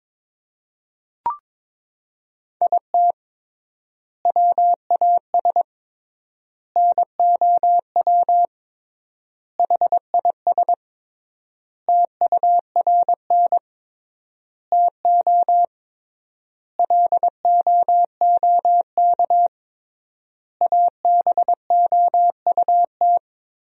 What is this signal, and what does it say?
Morse code, heard as a keyed tone, about 22 wpm, 700 Hz. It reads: IT WAS NOW HIS TURN TO LOOK ABOUT
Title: 3x Word-Spacing - Sentences from Top 200 Words - 22wpm